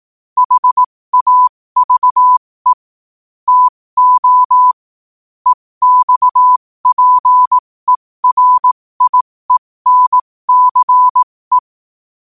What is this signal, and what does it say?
Morse code, heard as a keyed tone, about 18 wpm, spaced as for 15 wpm, 1000 Hz. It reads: HAVE TO EXPERIENCE
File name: qotdfiles / 15wpm